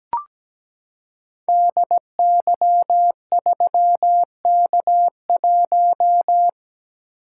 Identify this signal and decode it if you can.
Morse code, heard as a keyed tone, about 17 wpm, 700 Hz. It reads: DY3K1
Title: Code Groups - 5 Letter:Numbers - 17wpm